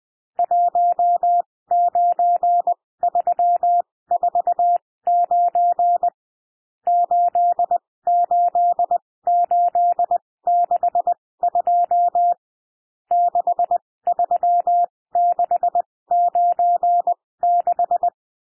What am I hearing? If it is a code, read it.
Morse code, heard as a keyed tone, about 20 wpm, spaced as for 16 wpm, 700 Hz. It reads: 19349 88862 63696
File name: cwmp3 / N1 SOUND NUMBERS 20 wpm